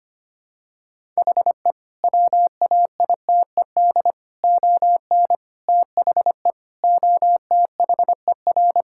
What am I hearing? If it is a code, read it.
Morse code, heard as a keyed tone, about 25 wpm, 700 Hz. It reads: HE WAITED ON THE OTHER
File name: practice / 25wpm / Sentences with 4-Word Phrases - 25wpm